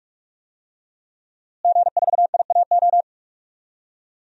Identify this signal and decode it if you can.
Morse code, heard as a keyed tone, about 46 wpm, 700 Hz. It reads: G4IAO
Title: DXCC-100 Call Signs - 5 Characters - Repeated 2x - 45wpm